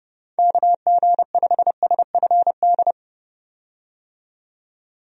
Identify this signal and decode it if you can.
Morse code, heard as a keyed tone, about 30 wpm, 700 Hz. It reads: KG5SFD
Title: USA Call Signs - 6 Characters - 30wpm